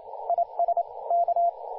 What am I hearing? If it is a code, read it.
Morse code, heard as a keyed tone, about 29 wpm, 690 Hz, with radio noise. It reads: IS K
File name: du1-ja1pbv-1-28-040mhz-cw-cqing